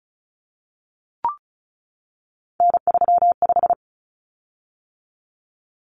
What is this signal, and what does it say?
Morse code, heard as a keyed tone, about 34 wpm, 700 Hz. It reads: N35
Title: Common Cut Numbers - 3-Digit 35wpm